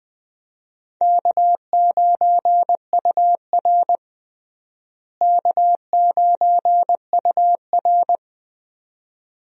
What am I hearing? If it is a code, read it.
Morse code, heard as a keyed tone, about 20 wpm, 700 Hz. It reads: K9UR K9UR
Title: USA Call Signs - 4 Characters - Repeated 2x - 20wpm